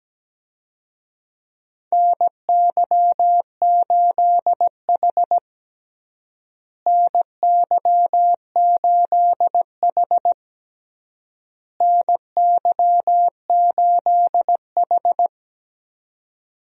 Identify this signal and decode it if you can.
Morse code, heard as a keyed tone, about 17 wpm, 700 Hz. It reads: NY8H NY8H NY8H
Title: USA Call Signs - 4 Characters - Repeated 3x - 17wpm